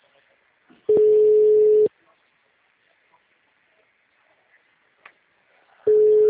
Ringback tone (Europe, 425 Hz).